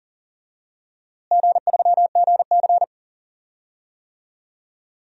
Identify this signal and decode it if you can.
Morse code, heard as a keyed tone, about 40 wpm, 690 Hz. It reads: G3GC
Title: DXCC-100 Call Signs - 4 Characters - Repeated 2x - 40wpm